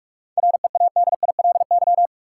Morse code, heard as a keyed tone, about 46 wpm, 700 Hz. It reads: READILY
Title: Seven Letter Words - 45wpm